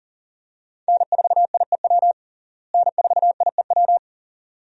Morse code, heard as a keyed tone, about 37 wpm, 700 Hz. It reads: N4IEW N4IEW
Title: USA Call Signs - 5 Characters - Repeated 2x - 40wpm